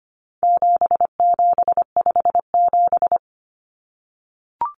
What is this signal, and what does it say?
Morse code, heard as a keyed tone, about 25 wpm, 700 Hz. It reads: 7757
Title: Numbers - 4-Digit 25wpm